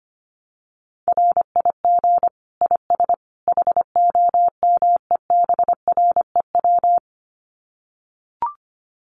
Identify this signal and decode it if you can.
Morse code, heard as a keyed tone, about 25 wpm, 700 Hz. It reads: RIG IS HOMEBREW
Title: Common QSO Phrases 25wpm